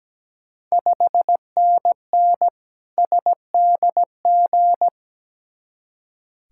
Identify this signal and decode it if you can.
Morse code, heard as a keyed tone, about 17 wpm, 700 Hz. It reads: 5NN SDG
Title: Ontario QSO Party Exchange 17wpm